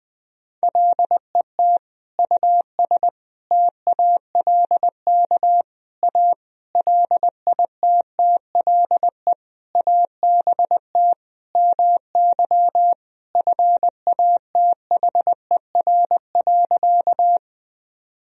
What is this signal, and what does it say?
Morse code, heard as a keyed tone, about 20 wpm, 700 Hz. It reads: LET US TALK A LITTLE ABT MY FATHER.
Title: Sentences from Top 400 Words and Abbreviations 20wpm